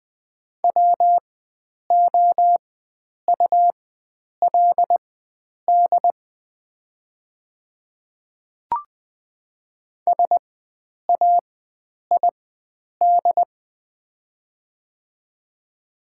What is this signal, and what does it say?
Morse code, heard as a keyed tone, about 21 wpm, spaced as for 9 wpm, 700 Hz. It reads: WOULD SAID
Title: 4x Character Spacing - Top 100 Words - 20wpm